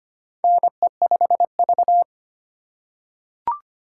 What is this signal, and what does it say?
Morse code, heard as a keyed tone, about 25 wpm, 700 Hz. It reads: NE5V
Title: USA Call Signs - 4x Character Spacing - 25wpm